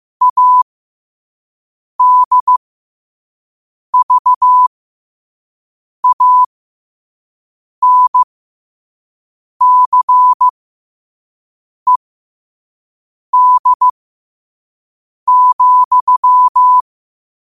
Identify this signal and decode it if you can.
Morse code, heard as a keyed tone, about 15 wpm, spaced as for 5 wpm, 1000 Hz. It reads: ADVANCED,
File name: qotdfiles / 5wpm